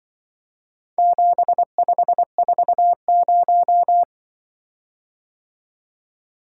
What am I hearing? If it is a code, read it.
Morse code, heard as a keyed tone, about 23 wpm, 700 Hz. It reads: 7540
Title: Speed-Racing - Four Letter-Number Combinations - 20wpm